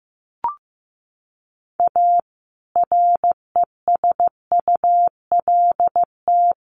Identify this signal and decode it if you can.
Morse code, heard as a keyed tone, about 15 wpm, 700 Hz. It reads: A RESULT
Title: Sets of 4 Words 15wpm